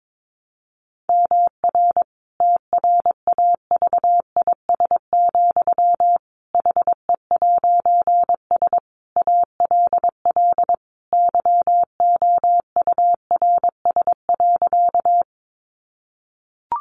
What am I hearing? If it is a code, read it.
Morse code, heard as a keyed tone, about 22 wpm, 700 Hz. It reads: MR TRAVIS, HE'S ALL YOURS.
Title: A Sound Of Thunder 22wpm